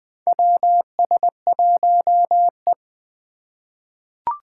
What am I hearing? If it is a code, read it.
Morse code, heard as a keyed tone, about 20 wpm, 700 Hz. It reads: WS1E
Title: USA Call Signs - 4 Characters - 2x Character Spacing - 20wpm